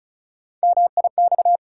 Morse code, heard as a keyed tone, about 36 wpm, 700 Hz. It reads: MIX
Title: Sentences from Top 1000 Words - Review Part 4 - New Word Drill - 35wpm